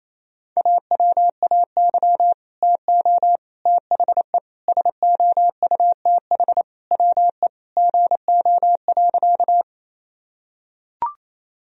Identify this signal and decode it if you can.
Morse code, heard as a keyed tone, about 28 wpm, 700 Hz. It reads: AWAY TO THE SOUTH WE GO.
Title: Sentences from Top 1000 Words - Review Part 2 - 28wpm